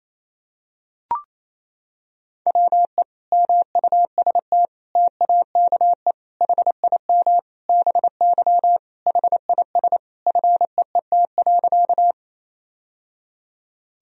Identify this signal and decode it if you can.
Morse code, heard as a keyed tone, about 28 wpm, 700 Hz. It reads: WE MUST TAKE HIM BY HIS FEET.
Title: Sentences from Top 1000 Words - Review Part 1 - 28wpm